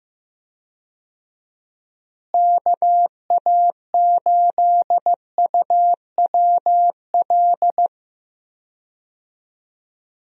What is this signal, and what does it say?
Morse code, heard as a keyed tone, about 15 wpm, 700 Hz. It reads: KA8UWL
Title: USA Call Signs - 6 Characters - Repeated 3x - 15wpm